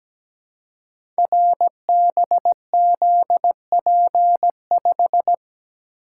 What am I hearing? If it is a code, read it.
Morse code, heard as a keyed tone, about 17 wpm, 700 Hz. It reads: RBZP5
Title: Code Groups - 5 Letter:Numbers - 17wpm